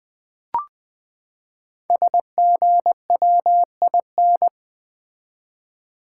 Morse code, heard as a keyed tone, about 20 wpm, 700 Hz. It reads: SGWIN
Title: Code Groups - 5 Letters - 20wpm